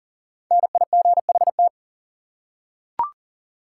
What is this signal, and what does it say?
Morse code, heard as a keyed tone, about 40 wpm, 700 Hz. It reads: NIGHT
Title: Five Letter Words - 40wpm